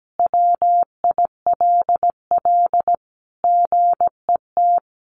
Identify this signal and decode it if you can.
Morse code, heard as a keyed tone, about 17 wpm, 700 Hz. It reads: WILL GET
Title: Additional Easy Phrases 17wpm